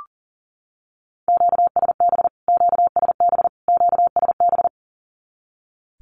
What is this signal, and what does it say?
Morse code, heard as a keyed tone, about 40 wpm, 700 Hz. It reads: QSB QSB QSB
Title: Q-Codes Repeated 3x 40wpm